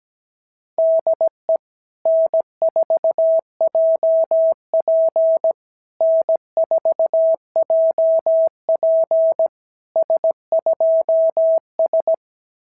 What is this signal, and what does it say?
Morse code, heard as a keyed tone, about 17 wpm, 650 Hz. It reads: DE N4JP N4JP S2S
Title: SOTA QSO Practice 17wpm